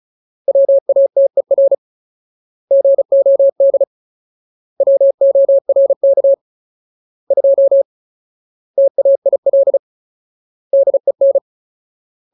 Morse code, heard as a keyed tone, about 34 wpm, 550 Hz. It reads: WATER GOD WORK 2 TAIL DEN
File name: Ultimate Word List / 35WPM Ultimate